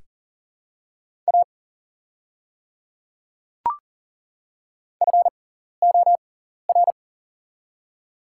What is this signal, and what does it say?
Morse code, heard as a keyed tone, about 40 wpm, spaced as for 14 wpm, 700 Hz. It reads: A FOR
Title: 6x Character Spacing - Top 100 Words - 40wpm